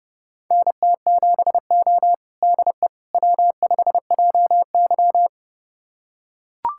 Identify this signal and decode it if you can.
Morse code, heard as a keyed tone, about 30 wpm, 700 Hz. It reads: NT7O DE W5JY
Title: USA Callsign DE Callsign - 4 Characters - 30wpm